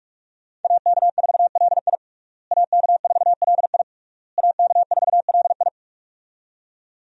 Morse code, heard as a keyed tone, about 46 wpm, 700 Hz. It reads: AK4LI AK4LI AK4LI
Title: USA Call Signs - 5 Characters - Repeated 3x - 45wpm